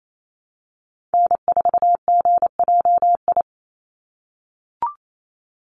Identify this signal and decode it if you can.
Morse code, heard as a keyed tone, about 28 wpm, 700 Hz. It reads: N4GJI
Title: USA Call Signs 28wpm